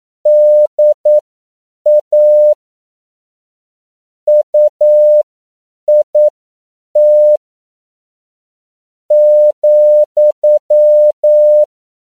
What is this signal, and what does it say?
Morse code, heard as a keyed tone, about 9 wpm, spaced as for 7 wpm, 600 Hz. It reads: DA UIT ,
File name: CW / 10PPM